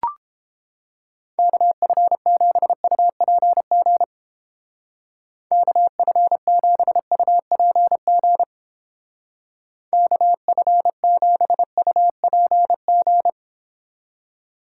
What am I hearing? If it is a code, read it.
Morse code, heard as a keyed tone, about 30 wpm, 700 Hz. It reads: KF7UPG KF7UPG KF7UPG